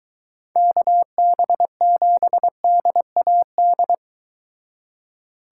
Morse code, heard as a keyed tone, about 23 wpm, 700 Hz. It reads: KB7DAD